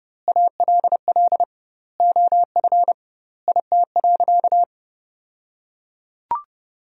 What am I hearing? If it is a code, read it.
Morse code, heard as a keyed tone, about 30 wpm, 700 Hz. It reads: ALL OF IT.
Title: 2x Word-Spacing - Sentences from Top 500 Words - 30wpm